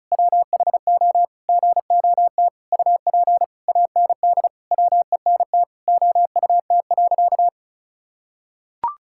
Morse code, heard as a keyed tone, about 34 wpm, 700 Hz. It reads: WHO GOT UP AND WENT OUT.
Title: Sentences from Top 1600 Words 35wpm